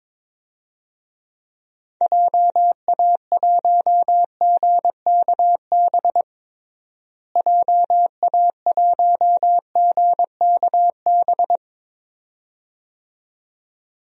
Morse code, heard as a keyed tone, about 21 wpm, 700 Hz. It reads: JA1GKB JA1GKB
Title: DXCC-100 Call Signs - 6 Characters - Repeated 2x - 22wpm